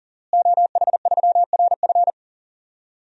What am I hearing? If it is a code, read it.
Morse code, heard as a keyed tone, about 40 wpm, 700 Hz. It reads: OH3RF
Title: DXCC-100 Call Signs - 5 Characters - Repeated 2x - 40wpm